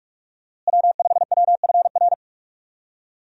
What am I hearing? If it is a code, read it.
Morse code, heard as a keyed tone, about 45 wpm, 700 Hz. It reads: W5WFR
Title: USA Call Signs - 5 Characters - Repeated 3x - 45wpm